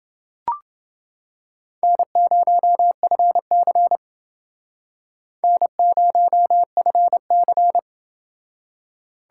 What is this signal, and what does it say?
Morse code, heard as a keyed tone, about 29 wpm, 700 Hz. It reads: N0FC N0FC